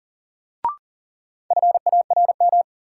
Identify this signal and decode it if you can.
Morse code, heard as a keyed tone, about 40 wpm, 700 Hz. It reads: FARM
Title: Sentences from Top 1200 Words - New Word Drill Alt - 40wpm